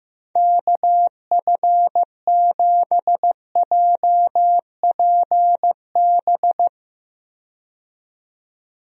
Morse code, heard as a keyed tone, about 15 wpm, 700 Hz. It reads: KF7JPB
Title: USA Call Signs - 6 Characters - 15wpm